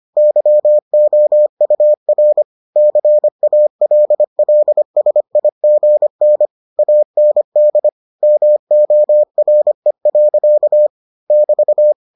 Morse code, heard as a keyed tone, about 25 wpm, 600 Hz. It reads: YOUR CALLSIGN AND MORE. <BT>